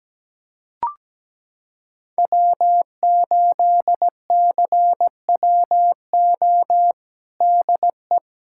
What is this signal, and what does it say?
Morse code, heard as a keyed tone, about 17 wpm, 700 Hz. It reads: W8CWO DE
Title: USA Callsign DE Callsign - 5 Characters - 17wpm